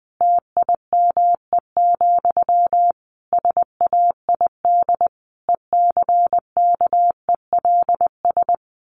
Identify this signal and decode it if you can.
Morse code, heard as a keyed tone, about 20 wpm, 700 Hz. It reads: TIME, SAID ECKELS
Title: A Sound Of Thunder 20wpm